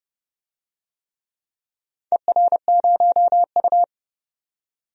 Morse code, heard as a keyed tone, about 28 wpm, 700 Hz. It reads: ER0U
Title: DXCC-100 Call Signs - 4 Characters - 30wpm